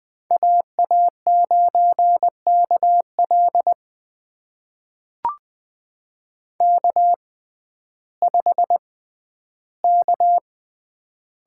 Morse code, heard as a keyed tone, about 20 wpm, spaced as for 12 wpm, 700 Hz. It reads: AA9KL K5K
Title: USA Call Signs - 5 Characters - 6x Character Spacing - 20wpm